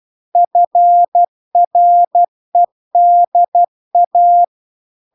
Morse code, heard as a keyed tone, about 12 wpm, 700 Hz. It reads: FREDA